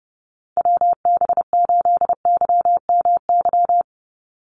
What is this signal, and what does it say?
Morse code, heard as a keyed tone, about 30 wpm, 700 Hz. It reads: WB8YMY
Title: USA Call Signs 30wpm